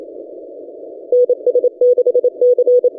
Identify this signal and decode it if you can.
Morse code, heard as a keyed tone, about 28 wpm, 490 Hz. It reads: NS6C
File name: ft5zm-17cw-2014-01-27-0120z-lp